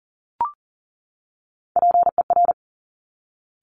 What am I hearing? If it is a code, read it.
Morse code, heard as a keyed tone, about 41 wpm, 690 Hz. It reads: PER
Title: Top Three Letter Combinations in Words 40wpm